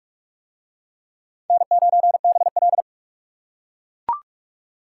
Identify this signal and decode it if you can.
Morse code, heard as a keyed tone, about 41 wpm, 700 Hz. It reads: N9BL